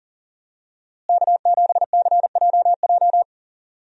Morse code, heard as a keyed tone, about 40 wpm, 700 Hz. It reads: K7CJJ
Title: USA Call Signs - 5 Characters - Repeated 3x - 40wpm